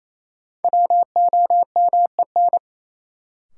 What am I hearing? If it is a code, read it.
Morse code, heard as a keyed tone, about 28 wpm, 700 Hz. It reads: WOMEN
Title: Sets of 3 Words - New Word Drill - 28wpm